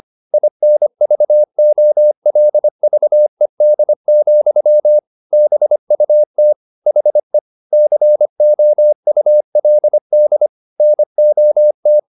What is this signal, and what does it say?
Morse code, heard as a keyed tone, about 25 wpm, 600 Hz. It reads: INVOLVED, BUT HE COULD NOT